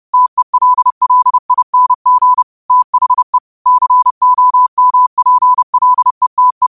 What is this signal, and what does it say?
Morse code, heard as a keyed tone, about 30 wpm, 1000 Hz. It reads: TELLING THE COMPLETE